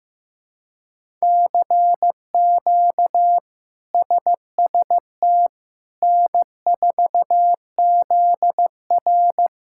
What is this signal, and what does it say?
Morse code, heard as a keyed tone, about 15 wpm, 700 Hz. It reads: CQ SST N4ZR
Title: K1USN SST QSO Practice - 15wpm